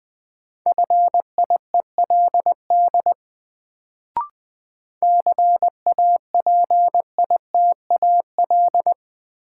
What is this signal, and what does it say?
Morse code, heard as a keyed tone, about 20 wpm, 700 Hz. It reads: FIELD CAPITAL
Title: Sentences from Top 1000 Words - New Word Drill - 20wpm